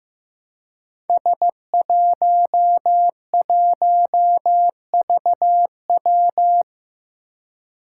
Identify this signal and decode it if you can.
Morse code, heard as a keyed tone, about 15 wpm, 700 Hz. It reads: S11VW
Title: Five Challenging Letter-Number Combinations 15wpm